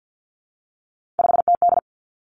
Morse code, heard as a keyed tone, about 50 wpm, 700 Hz. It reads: 5TD
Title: Three Challenging Letter-Number Combinations 50wpm